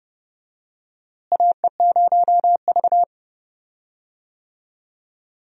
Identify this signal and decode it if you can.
Morse code, heard as a keyed tone, about 30 wpm, 700 Hz. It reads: AE0V
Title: USA Call Signs - 4 Characters - Repeated 3x - 30wpm